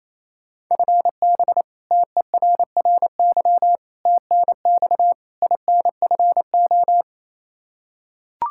Morse code, heard as a keyed tone, about 27 wpm, 700 Hz. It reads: FB TERRY TNX INFO
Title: Practice QSO - Exchange 3 - More Info - 28wpm